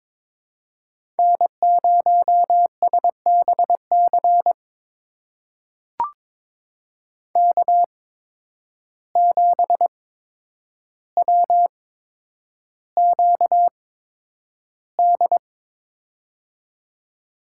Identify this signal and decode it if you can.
Morse code, heard as a keyed tone, about 22 wpm, spaced as for 9 wpm, 700 Hz. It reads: N0SBC K7WQD
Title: USA Call Signs - 8x Character Spacing - 22wpm